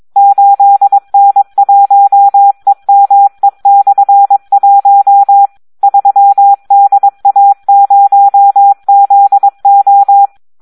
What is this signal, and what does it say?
Morse code, heard as a keyed tone, about 22 wpm, 800 Hz. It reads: 8N1EME/1 3DA0ZO
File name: Audio practica CW / SD 1 22